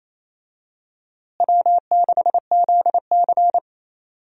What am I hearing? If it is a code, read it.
Morse code, heard as a keyed tone, about 28 wpm, 700 Hz. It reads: W6ZC
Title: USA Call Signs - 4 Characters - Repeated 2x - 28wpm